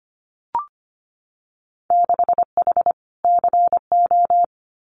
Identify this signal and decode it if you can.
Morse code, heard as a keyed tone, about 25 wpm, 700 Hz. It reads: 6H CO
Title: Winter Field Day Exchanges Encore 25wpm